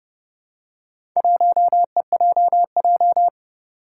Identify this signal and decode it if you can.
Morse code, heard as a keyed tone, about 29 wpm, 700 Hz. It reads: 1EJJ
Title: Four Challenging Letter-Number Combinations 30wpm